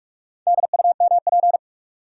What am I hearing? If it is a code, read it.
Morse code, heard as a keyed tone, about 47 wpm, 700 Hz. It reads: DUMP
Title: Sentences from Top 1300 Words - New Word Drill Alt - 45wpm